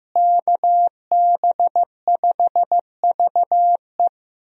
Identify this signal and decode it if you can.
Morse code, heard as a keyed tone, about 15 wpm, 700 Hz. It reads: KB5VE
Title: USA Call Signs - 5 Characters - Repeated 3x - 15wpm